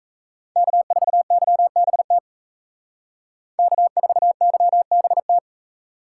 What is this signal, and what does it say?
Morse code, heard as a keyed tone, about 36 wpm, 700 Hz. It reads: K4YBT K4YBT